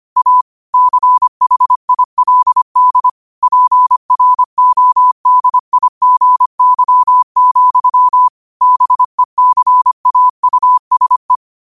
Morse code, heard as a keyed tone, about 25 wpm, 1000 Hz. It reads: A CHILD PRODIGY, BECAUSE
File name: 25wpm